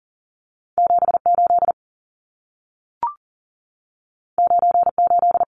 Morse code, heard as a keyed tone, about 40 wpm, 700 Hz. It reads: 78 98
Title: Numbers - 2-Digit 40wpm